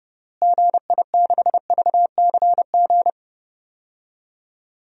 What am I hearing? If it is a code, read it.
Morse code, heard as a keyed tone, about 32 wpm, 700 Hz. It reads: GI6VCG
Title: DXCC-100 Call Signs - 6 Characters - Repeated 2x - 30wpm